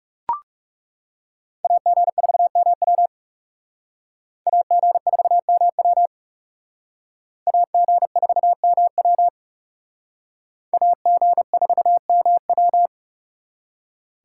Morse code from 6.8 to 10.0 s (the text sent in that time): AG4MW